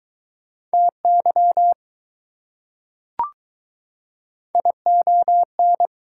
Morse code, heard as a keyed tone, about 21 wpm, 690 Hz. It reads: TY ION